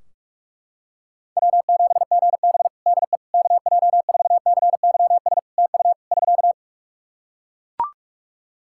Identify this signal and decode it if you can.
Morse code, heard as a keyed tone, about 43 wpm, 700 Hz. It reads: W7GB DE KJ4CYS TU <SK>
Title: Practice QSO - Exchange 4 - Ending - 45wpm